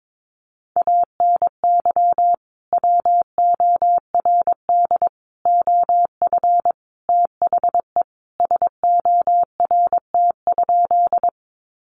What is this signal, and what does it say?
Morse code, heard as a keyed tone, about 22 wpm, 700 Hz. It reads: ANY WORD OF THE SORT?
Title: Sentences from Top 1300 Words 22wpm